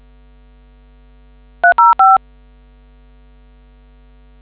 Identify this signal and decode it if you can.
DTMF, heard as telephone touch tones, tones of about 170 ms. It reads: 305